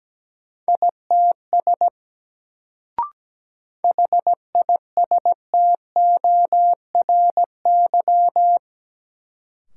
Morse code, heard as a keyed tone, about 17 wpm, 700 Hz. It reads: ITS HISTORY
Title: Sentences from Top 600 Words - New Word Drill - 17wpm